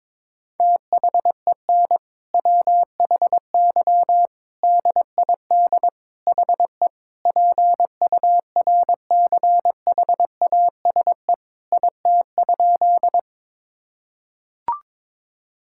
Morse code, heard as a keyed tone, about 21 wpm, 700 Hz. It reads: THEN WHY DID HE PURCHASE IT?
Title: Sentences from Top 1000 Words - Review Part 5 - 22wpm